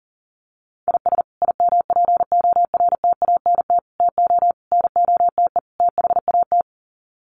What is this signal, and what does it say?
Morse code, heard as a keyed tone, about 40 wpm, 700 Hz. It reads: IS IMPORTANT TO NOTE THAT
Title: Sets of 5 Words 40wpm